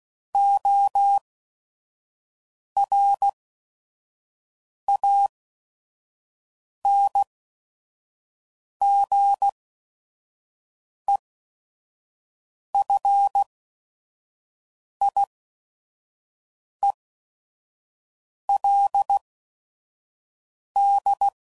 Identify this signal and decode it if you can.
Morse code, heard as a keyed tone, about 16 wpm, spaced as for 5 wpm, 800 Hz. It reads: ORANGEFIELD
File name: CW Code Groups @5WPM / fromSM's / Cities